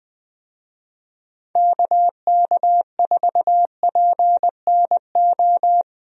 Morse code, heard as a keyed tone, about 20 wpm, 700 Hz. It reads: KK4PNO
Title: USA Call Signs - 6 Characters - 20wpm